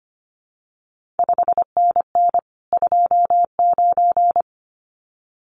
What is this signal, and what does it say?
Morse code, heard as a keyed tone, about 25 wpm, 700 Hz. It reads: 5NN 29
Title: CQ WW DX Contest Exchange Encore - 25wpm